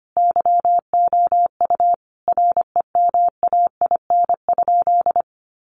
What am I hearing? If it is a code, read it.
Morse code, heard as a keyed tone, about 25 wpm, 700 Hz. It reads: YOU REMAIN?
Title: Sentences from Top 1200 Words 25wpm